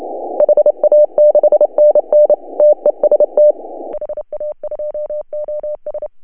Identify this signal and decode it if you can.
Morse code, heard as a keyed tone, about 28 wpm, 590 Hz. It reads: HA6NN TEST HA2OS
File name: CW / HA2OS RHOB 2017 CW